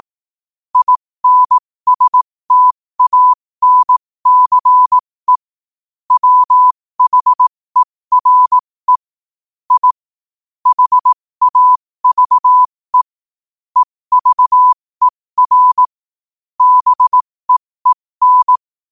Morse code, heard as a keyed tone, about 18 wpm, spaced as for 15 wpm, 1000 Hz. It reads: INSTANCE WHERE I HAVE EVER BEEN
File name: 15wpm